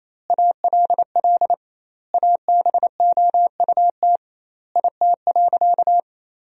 Morse code, heard as a keyed tone, about 28 wpm, 700 Hz. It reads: ALL ABOUT IT.
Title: 2x Word-Spacing - Sentences from Top 400 Words - 28wpm